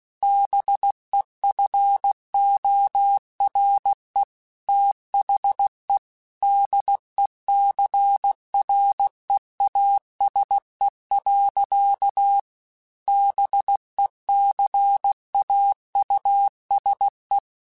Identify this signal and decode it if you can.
Morse code, heard as a keyed tone, about 16 wpm, 800 Hz. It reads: BEFORE THE DECREASE. BECAUSE
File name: Technical Subjects / Technical Subjects-mp3s / XC @15